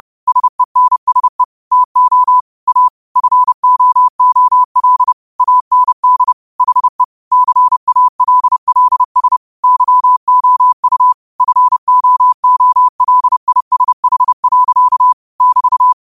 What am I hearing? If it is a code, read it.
Morse code, heard as a keyed tone, about 30 wpm, 1000 Hz. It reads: SENSE TO A FOOL AND HE CALLS YOU FOOLISH. <BT>
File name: qotdfiles / 30wpm